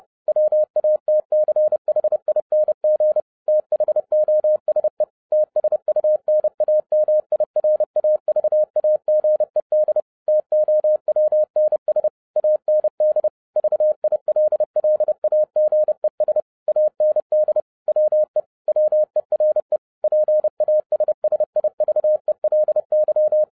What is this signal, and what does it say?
Morse code, heard as a keyed tone, about 30 wpm, 610 Hz. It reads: WATCHING THOSE TSUNAMI RAVAGED TOWNS AND VILLAGES AND WE WERE PASSIVELY